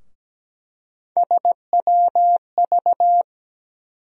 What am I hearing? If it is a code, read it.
Morse code, heard as a keyed tone, about 17 wpm, 700 Hz. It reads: SWV
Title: USA Call Sign Postfixes - 3 Letters 17wpm